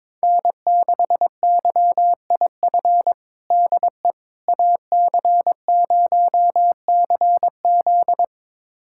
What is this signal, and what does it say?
Morse code, heard as a keyed tone, about 22 wpm, 700 Hz. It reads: N6YIF DE AC0CZ
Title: USA Callsign DE Callsign - 5 Characters - 22wpm